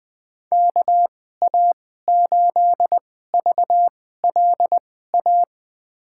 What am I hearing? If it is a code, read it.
Morse code, heard as a keyed tone, about 20 wpm, spaced as for 14 wpm, 700 Hz. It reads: KA8VLA
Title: USA Call Signs - 6 Characters - 2x Character Spacing - 20wpm